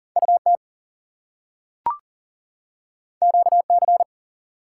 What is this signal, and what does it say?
Morse code, heard as a keyed tone, about 41 wpm, 700 Hz. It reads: UT QC
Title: US States and Canadian Provinces 40wpm